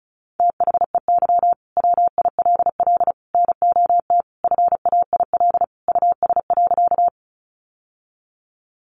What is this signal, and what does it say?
Morse code, heard as a keyed tone, about 34 wpm, 700 Hz. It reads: THEY WILL NOT FAIL US.
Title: Sentences from Top 1100 Words 35wpm